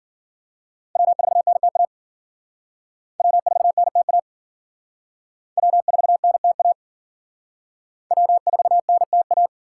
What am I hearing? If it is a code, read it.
Morse code, heard as a keyed tone, about 50 wpm, 700 Hz. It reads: W4NTA W4NTA W4NTA W4NTA
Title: Speed Racing - USA Call Signs 40wpm